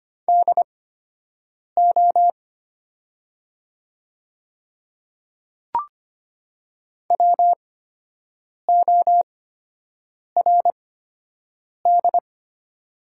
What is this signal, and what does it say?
Morse code, heard as a keyed tone, about 25 wpm, spaced as for 7 wpm, 700 Hz. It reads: DO WORD
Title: 8x Character Spacing - Top 100 Words - 25wpm